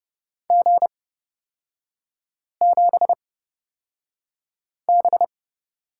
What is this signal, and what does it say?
Morse code, heard as a keyed tone, about 29 wpm, spaced as for 5 wpm, 700 Hz. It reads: G7B
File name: Lesson 36e - BT - Three Characters 30wpm